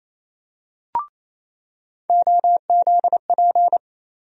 Morse code, heard as a keyed tone, about 28 wpm, 700 Hz. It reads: OZP
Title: USA Call Sign Postfixes - 3 Letters 28wpm